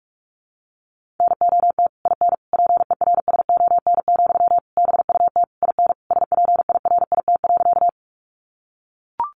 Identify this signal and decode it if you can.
Morse code, heard as a keyed tone, about 44 wpm, 700 Hz. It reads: NOT IN PERSON, BUT IN SPIRIT.
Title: Sentences from Top 1400 Words 45wpm